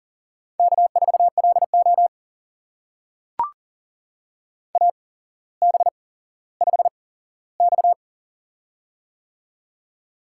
Morse code, heard as a keyed tone, about 37 wpm, spaced as for 17 wpm, 700 Hz. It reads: K4LO AB5X
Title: USA Call Signs - 8x Character Spacing - 40wpm